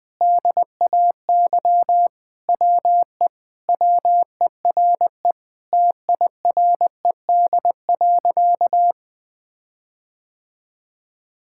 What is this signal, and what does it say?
Morse code, heard as a keyed tone, about 20 wpm, 700 Hz. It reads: DAY WE WERE TIRED.
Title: Sentences with 5-Word Phrases - 20wpm